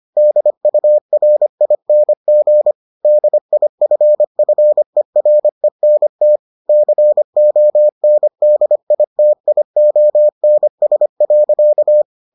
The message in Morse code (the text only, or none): DURING DIFFERENT CONDITIONS.